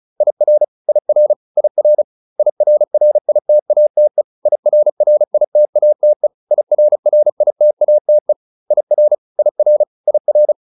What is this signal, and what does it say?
Morse code, heard as a keyed tone, about 35 wpm, 600 Hz. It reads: IR IR IR IRRITATE IRRITATE IRRITATE IR IR IR
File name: IR8 35